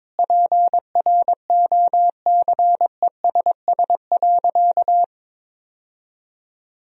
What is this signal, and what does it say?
Morse code, heard as a keyed tone, about 22 wpm, 700 Hz. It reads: PROCESS.
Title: Sentences from Top 1000 Words - Review Part 3 - 22wpm